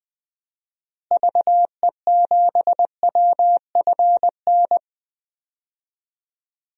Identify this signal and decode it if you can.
Morse code, heard as a keyed tone, about 20 wpm, 700 Hz. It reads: VE7WFN
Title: DXCC-100 Call Signs - 6 Characters - Repeated 2x - 20wpm